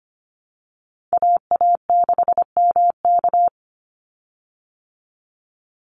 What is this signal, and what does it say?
Morse code, heard as a keyed tone, about 25 wpm, 700 Hz. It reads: AA6MK
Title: Call Signs Repeated 3x - 25wpm